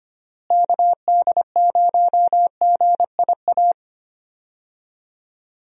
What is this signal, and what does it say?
Morse code, heard as a keyed tone, about 25 wpm, 700 Hz. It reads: KD0GIA
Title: USA Call Signs - 6 Characters - Repeated 3x - 25wpm